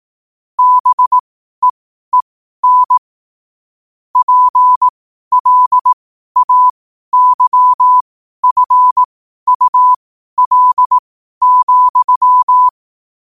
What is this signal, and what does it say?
Morse code, heard as a keyed tone, about 18 wpm, spaced as for 12 wpm, 1000 Hz. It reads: BEEN PLAYFUL,
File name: qotdfiles / 12wpm